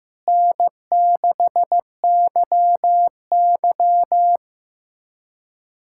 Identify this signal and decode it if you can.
Morse code, heard as a keyed tone, about 15 wpm, 700 Hz. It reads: N6YY
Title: USA Callsign DE Callsign - 4 Characters - 15wpm